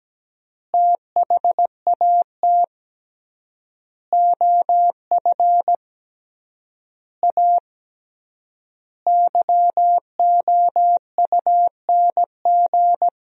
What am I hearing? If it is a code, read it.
Morse code, heard as a keyed tone, about 17 wpm, 700 Hz. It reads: THAT OF A YOUNG
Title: 3x Word-Spacing - Sentences from Top 500 Words - 17wpm